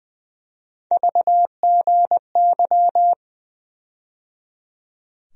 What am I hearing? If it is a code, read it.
Morse code, heard as a keyed tone, about 20 wpm, 700 Hz. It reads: VGY
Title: USA Call Sign Postfixes - 3 Letters 20wpm